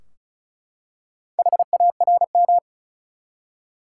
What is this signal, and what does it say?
Morse code, heard as a keyed tone, about 36 wpm, 700 Hz. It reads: HARM
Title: Four Letter Words - 35wpm